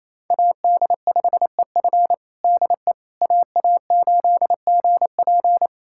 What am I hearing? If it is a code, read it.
Morse code, heard as a keyed tone, about 28 wpm, 700 Hz. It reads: AD5EF DE AA8GP
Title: USA Callsign DE Callsign - 5 Characters - 28wpm